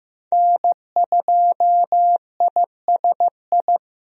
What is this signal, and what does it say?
Morse code, heard as a keyed tone, about 15 wpm, 700 Hz. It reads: N2ISI
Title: USA Call Signs - 5 Characters - Repeated 3x - 15wpm